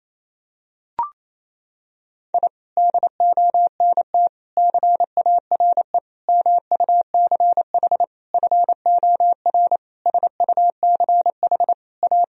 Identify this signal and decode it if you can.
Morse code, heard as a keyed tone, about 28 wpm, 700 Hz. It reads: I DONT CARE MUCH FOR SUCH A
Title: Sentences from Top 1000 Words - Review Part 4 - 28wpm